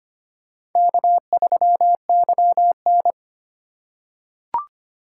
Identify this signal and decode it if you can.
Morse code, heard as a keyed tone, about 25 wpm, 700 Hz. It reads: K3YN
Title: USA Call Signs - 3x Character Spacing - 25wpm